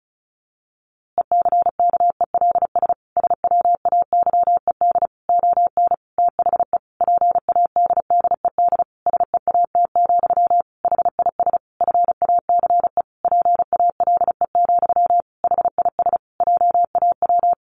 Morse code, heard as a keyed tone, about 34 wpm, 700 Hz. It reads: ECKELS SWAYED ON THE PADDED SEAT, HIS FACE PALE, HIS JAW
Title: A Sound Of Thunder 35wpm